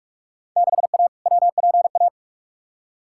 Morse code, heard as a keyed tone, about 44 wpm, 700 Hz. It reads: 6A WPA
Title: Field Day Exchanges Encore - 45wpm